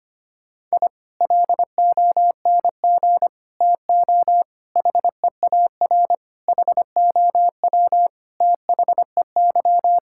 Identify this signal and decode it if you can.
Morse code, heard as a keyed tone, about 25 wpm, 700 Hz. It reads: I LONG TO HEAR HOW THEY
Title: Sentences from Top 1000 Words - Review Part 5 - 25wpm